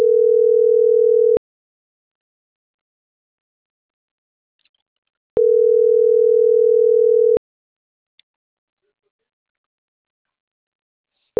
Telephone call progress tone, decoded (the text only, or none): ringback tone